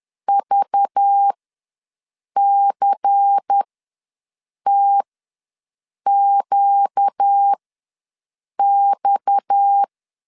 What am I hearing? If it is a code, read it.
Morse code, heard as a keyed tone, about 10 wpm, spaced as for 6 wpm, 800 Hz. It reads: VCTQX
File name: LEZIONI / 6 PKX0 / 6 PKX ws 10wpm 3706